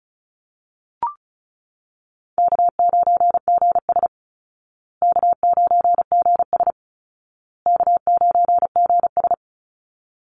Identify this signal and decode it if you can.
Morse code, heard as a keyed tone, about 34 wpm, 700 Hz. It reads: K9GS K9GS K9GS
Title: Call Signs Repeated 2x - 35wpm